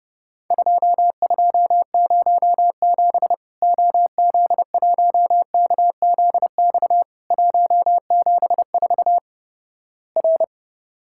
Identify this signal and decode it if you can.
Morse code, heard as a keyed tone, about 32 wpm, 700 Hz. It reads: 2207 OZ1KZX 174 R
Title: Worked All Europe DX Contest - QTC Exchange - 30wpm